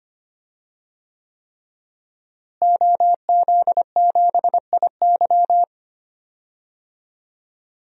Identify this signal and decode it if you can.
Morse code, heard as a keyed tone, about 25 wpm, 700 Hz. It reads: OZ7IY
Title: DXCC-100 Call Signs - 5 Characters - Repeated 3x - 25wpm